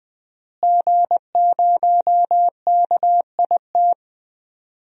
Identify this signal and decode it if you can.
Morse code, heard as a keyed tone, about 20 wpm, 700 Hz. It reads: G0KIT
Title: DXCC-100 Call Signs - 5 Characters - Repeated 2x - 20wpm